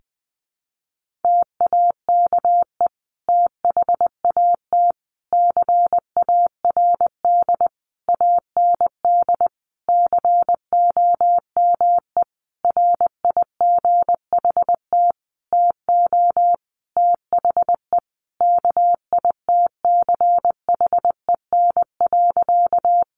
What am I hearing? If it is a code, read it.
Morse code, heard as a keyed tone, about 20 wpm, 700 Hz. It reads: TAKE THAT CARD AND COME RIGHT TO THE KITCHEN.
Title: Sentences from Top 1400 Words 20wpm